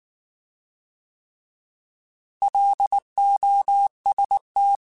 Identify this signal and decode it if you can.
Morse code, heard as a keyed tone, about 19 wpm, 800 Hz. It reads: LOST
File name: Alphabetical / Alphabetical mp3s / Ls @18